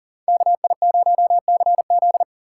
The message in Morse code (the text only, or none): KI0CZ